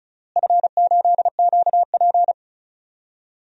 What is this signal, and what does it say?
Morse code, heard as a keyed tone, about 34 wpm, 700 Hz. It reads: F8QP
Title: DXCC-100 Call Signs - 4 Characters - Repeated 3x - 35wpm